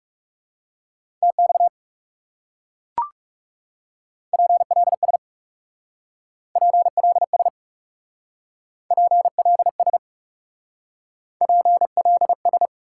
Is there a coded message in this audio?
Yes: Morse code from 1.2 to 12.7 s, heard as a keyed tone, about 40 wpm, 700 Hz. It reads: TX PLS PLS PLS PLS